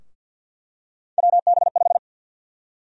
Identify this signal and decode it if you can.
Morse code, heard as a keyed tone, about 48 wpm, 700 Hz. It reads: WB5
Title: USA Call Sign Prefixes - 50wpm